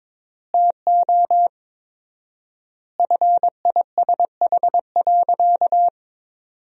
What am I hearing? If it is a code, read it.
Morse code, heard as a keyed tone, about 22 wpm, 700 Hz. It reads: TO FISH.
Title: 4x Word-Spacing - Sentences from Top 300 Words - 22wpm